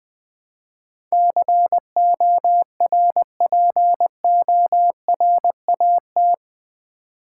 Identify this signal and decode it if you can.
Morse code, heard as a keyed tone, about 20 wpm, 700 Hz. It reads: CORPORAT
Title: Build A Word - 4 Letters At A Time - 11-Letter Words - 20wpm